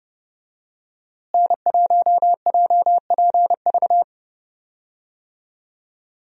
Morse code, heard as a keyed tone, about 30 wpm, 700 Hz. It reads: N1JPV
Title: USA Call Signs - 5 Characters - Repeated 3x - 30wpm